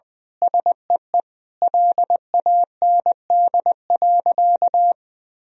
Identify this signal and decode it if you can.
Morse code, heard as a keyed tone, about 21 wpm, 700 Hz. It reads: SEE LAND.
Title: Sentences from Top 1000 Words - Review Part 2 - 20wpm